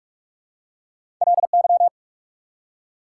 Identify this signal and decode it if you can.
Morse code, heard as a keyed tone, about 44 wpm, 690 Hz. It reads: LY
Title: Prefixes and Suffixes 45wpm